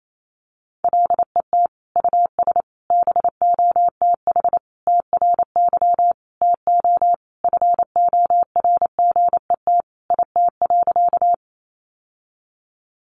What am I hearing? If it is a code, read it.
Morse code, heard as a keyed tone, about 28 wpm, 700 Hz. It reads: LET US BOTH TRY TO FORGET IT.
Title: Sentences from Top 1200 Words 28wpm